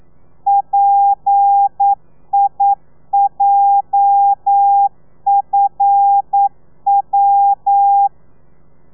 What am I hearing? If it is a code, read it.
Morse code, heard as a keyed tone, about 9 wpm, 800 Hz. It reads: PIJFW
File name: telegrafia / i2tfj / Lez8 40